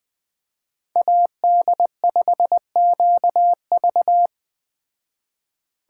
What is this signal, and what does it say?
Morse code, heard as a keyed tone, about 20 wpm, 700 Hz. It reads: AD5QV
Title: Speed Racing - USA 2x2 Call Signs 17wpm